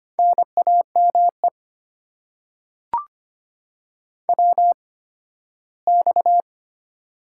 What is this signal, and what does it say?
Morse code, heard as a keyed tone, about 25 wpm, spaced as for 15 wpm, 700 Hz. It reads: NAME WX
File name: Top 90 QSO Elements - 8x Character Spacing - 25wpm